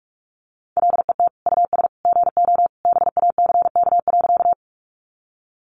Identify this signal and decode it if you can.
Morse code, heard as a keyed tone, about 44 wpm, 700 Hz. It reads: LET US GO BACK.
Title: Sentences from Top 300 Words 45wpm